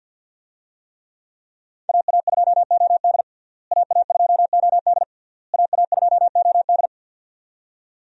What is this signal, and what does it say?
Morse code, heard as a keyed tone, about 51 wpm, 700 Hz. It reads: AA2OD AA2OD AA2OD
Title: USA Call Signs - 5 Characters - Repeated 3x - 50wpm